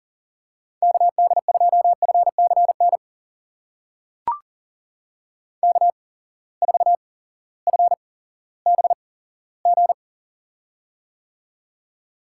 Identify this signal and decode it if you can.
Morse code, heard as a keyed tone, about 40 wpm, spaced as for 18 wpm, 700 Hz. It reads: KD2FCN K4FBG
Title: USA Call Signs - 8x Character Spacing - 40wpm